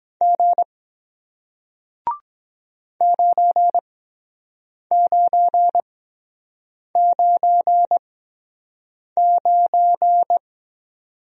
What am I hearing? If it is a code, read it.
Morse code, heard as a keyed tone, about 23 wpm, spaced as for 7 wpm, 700 Hz. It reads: G 9999